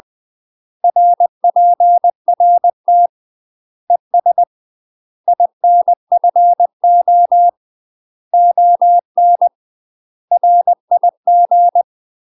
Morse code, heard as a keyed tone, about 20 wpm, 700 Hz. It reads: RPRT ES INFO ON RIG
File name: UK MASTER QSOs TRAINING / 20WPM / UK MASTER QSOs 0040